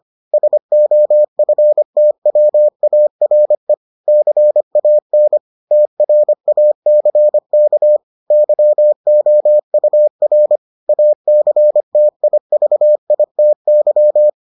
Morse code, heard as a keyed tone, about 25 wpm, 600 Hz. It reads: SOFTWARE CAN TRACK YOUR ACTIVITY